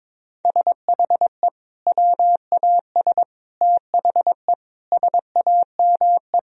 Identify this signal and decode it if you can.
Morse code, heard as a keyed tone, about 22 wpm, 700 Hz. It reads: SHE WAS THE SAME
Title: Sentences from Top 1000 Words - Review Part 3 - 22wpm